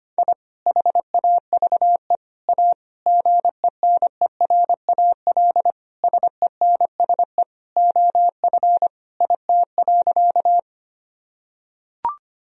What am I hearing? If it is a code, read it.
Morse code, heard as a keyed tone, about 25 wpm, 700 Hz. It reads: I HAVE A GENERAL SENSE OF IT.
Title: Sentences from Top 1000 Words - Review Part 4 - 25wpm